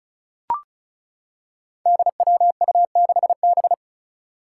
Morse code, heard as a keyed tone, about 36 wpm, 700 Hz. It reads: DWU6B
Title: Five Challenging Letter-Number Combinations 35wpm